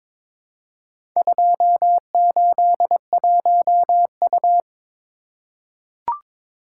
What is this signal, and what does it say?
Morse code, heard as a keyed tone, about 22 wpm, 700 Hz. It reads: 281U
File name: practice / 22wpm / Four Letter-Number Combinations - 22wpm